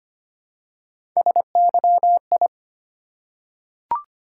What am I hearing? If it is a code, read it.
Morse code, heard as a keyed tone, about 25 wpm, 700 Hz. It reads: SYI
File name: Three Letter-Number Combinations - 25wpm